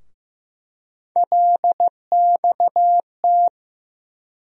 Morse code, heard as a keyed tone, about 15 wpm, 700 Hz. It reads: LXT
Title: USA Call Sign Postfixes - 3 Letters 15wpm